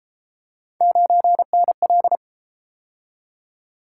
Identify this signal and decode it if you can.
Morse code, heard as a keyed tone, about 32 wpm, 700 Hz. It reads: 9NL